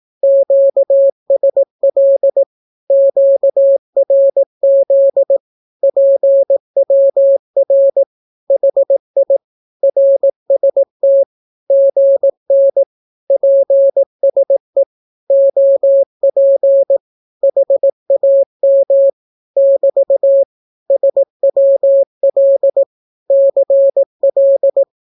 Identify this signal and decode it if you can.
Morse code, heard as a keyed tone, about 18 wpm, 550 Hz. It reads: QSL QRZ PWR HI RST GN PSE OP HAM <BT> SWL CL
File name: naukapodstaw / skroty / skroty